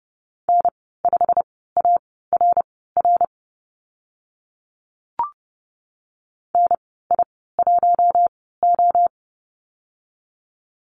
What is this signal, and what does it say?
Morse code, heard as a keyed tone, about 32 wpm, spaced as for 17 wpm, 700 Hz. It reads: N5ARR NI1O
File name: Lesson 15g - Number 5 - Call Signs 30wpm